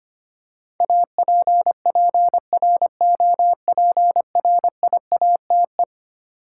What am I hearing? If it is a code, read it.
Morse code, heard as a keyed tone, about 25 wpm, 700 Hz. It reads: APPROPRIATE
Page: morse-code-ninja-practice-25wpm